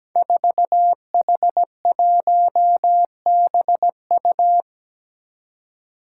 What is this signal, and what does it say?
Morse code, heard as a keyed tone, about 17 wpm, 700 Hz. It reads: 4H1BU
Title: Five Challenging Letter-Number Combinations 17wpm